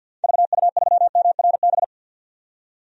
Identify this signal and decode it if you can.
Morse code, heard as a keyed tone, about 47 wpm, 700 Hz. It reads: VU3MRB
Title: DXCC-100 Call Signs - 6 Characters - 50wpm